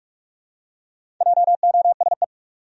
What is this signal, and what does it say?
Morse code, heard as a keyed tone, about 45 wpm, 700 Hz. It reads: JOSE
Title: USA Common Names - Max 5 Letters - 45wpm